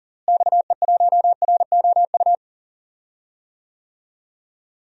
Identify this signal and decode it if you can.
Morse code, heard as a keyed tone, about 40 wpm, 700 Hz. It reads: XE1ROU
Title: DXCC-100 Call Signs - 6 Characters - Repeated 3x - 40wpm